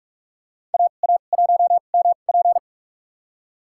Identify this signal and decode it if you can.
Morse code, heard as a keyed tone, about 41 wpm, spaced as for 31 wpm, 700 Hz. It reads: AA1MP
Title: USA Call Signs - 5 Characters - 2x Character Spacing - 45wpm